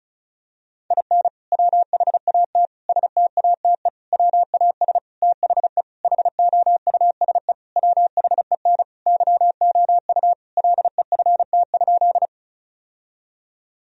Morse code, heard as a keyed tone, about 36 wpm, 700 Hz. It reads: IN WHAT STATE WAS THE HOUSE WHEN YOU LEFT?
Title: Sentences from Top 1000 Words - Review Part 2 - 35wpm